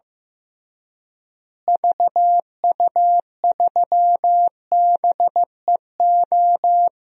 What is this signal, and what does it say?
Morse code, heard as a keyed tone, about 15 wpm, 700 Hz. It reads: VU3BEO
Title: DXCC-100 Call Signs - 6 Characters - Repeated 2x - 15wpm